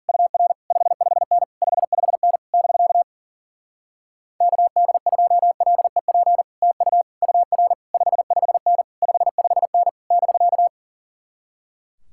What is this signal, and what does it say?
Morse code, heard as a keyed tone, about 52 wpm, 700 Hz. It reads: UR 55N 55N <BK> KD2LEP TU UR 55N 55N <BK>